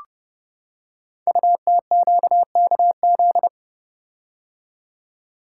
Morse code, heard as a keyed tone, about 28 wpm, 700 Hz. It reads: UTQKZ
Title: Code Groups - 5 Letters - 30wpm